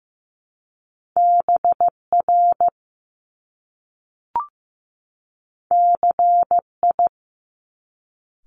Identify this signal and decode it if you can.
Morse code, heard as a keyed tone, about 15 wpm, 700 Hz. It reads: BR CI
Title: Top Two Letter Combinations in Words 15wpm